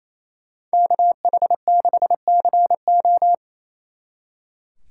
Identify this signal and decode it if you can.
Morse code, heard as a keyed tone, about 33 wpm, 700 Hz. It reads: KH6CO
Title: Speed Racing - USA 2x2 Call Signs 28wpm